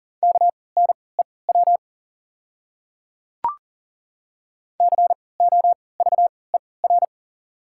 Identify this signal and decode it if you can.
Morse code, heard as a keyed tone, about 40 wpm, spaced as for 23 wpm, 700 Hz. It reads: KNEW COVER
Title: 3x Character Spacing - Top 500 Words - 40wpm